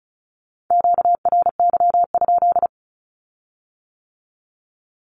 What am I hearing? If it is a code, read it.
Morse code, heard as a keyed tone, about 36 wpm, 700 Hz. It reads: QRY?
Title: Q-Codes with Spoken Meaning - 35wpm